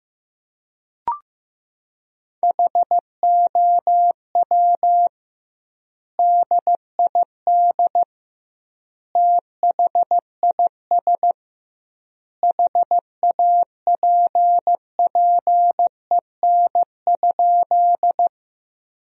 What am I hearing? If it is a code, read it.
Morse code, heard as a keyed tone, about 15 wpm, 700 Hz. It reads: HOW DID THIS HAPPEN?
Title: 2x Word-Spacing - Sentences from Top 400 Words - 15wpm